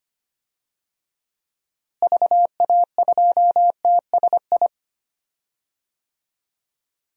Morse code, heard as a keyed tone, about 25 wpm, 700 Hz. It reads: VA2TSI
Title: DXCC-100 Call Signs - 6 Characters - Repeated 2x - 25wpm